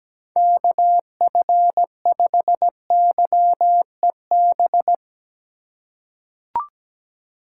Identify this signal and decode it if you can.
Morse code, heard as a keyed tone, about 17 wpm, 700 Hz. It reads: KF5YEB
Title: USA Call Signs - 6 Characters - 3x Character Spacing - 17wpm